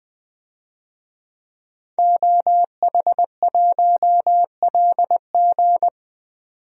Morse code, heard as a keyed tone, about 20 wpm, 700 Hz. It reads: OH1LG
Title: DXCC-100 Call Signs - 5 Characters - 20wpm